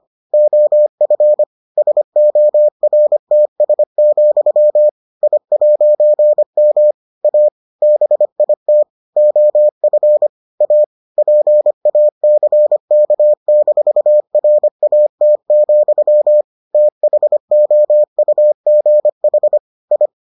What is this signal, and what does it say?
Morse code, heard as a keyed tone, about 25 wpm, 600 Hz. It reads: OF SORTS, I'M A BIT OF A PACK-RAT, THOUGH I